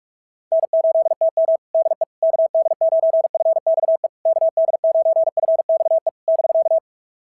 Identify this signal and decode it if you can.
Morse code, heard as a keyed tone, about 46 wpm, 690 Hz. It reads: N8TM DE KD9FXE KD9FXE <BK>
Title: SOTA QSO Practice 45wpm